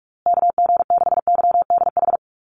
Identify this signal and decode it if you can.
Morse code, heard as a keyed tone, about 46 wpm, 700 Hz. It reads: KG6YDH
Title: USA Call Signs Repeated 3x 45wpm